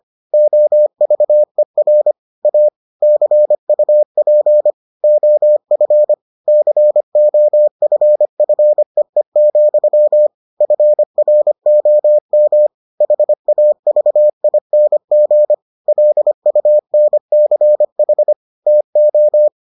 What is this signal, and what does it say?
Morse code, heard as a keyed tone, about 25 wpm, 600 Hz. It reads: OVER A CUP OF COFFEE, FROM HAVING LUNCH TO